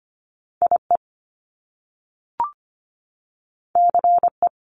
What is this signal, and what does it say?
Morse code, heard as a keyed tone, about 25 wpm, 700 Hz. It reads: IE CE
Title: Top Two Letter Combinations in Words 25wpm